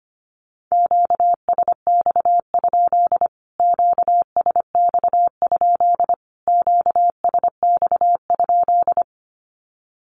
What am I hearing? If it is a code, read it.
Morse code, heard as a keyed tone, about 25 wpm, 700 Hz. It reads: QSX? QSX? QSX?
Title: Q-Codes Repeated 3x 25wpm